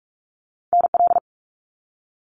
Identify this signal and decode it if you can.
Morse code, heard as a keyed tone, about 46 wpm, 700 Hz. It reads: NL
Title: Canadian Provinces and Territories 45wpm